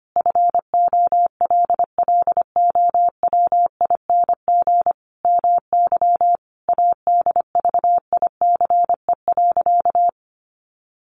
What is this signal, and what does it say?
Morse code, heard as a keyed tone, about 25 wpm, 700 Hz. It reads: FOLLOWING MY ADVICE.
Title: Sentences from Top 1100 Words 25wpm